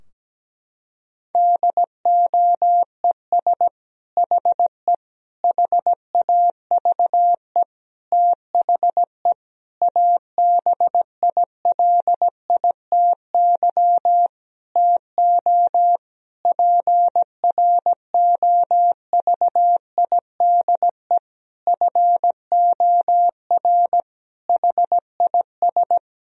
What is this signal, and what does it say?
Morse code, heard as a keyed tone, about 17 wpm, 700 Hz. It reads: DOES HE HAVE THE ABILITY TO PROVIDE FOR HIS
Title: Sentences from Top 1000 Words - Review Part 3 - 17wpm